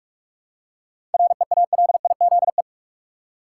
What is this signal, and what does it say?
Morse code, heard as a keyed tone, about 47 wpm, 700 Hz. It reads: REALIZE
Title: Seven Letter Words - 45wpm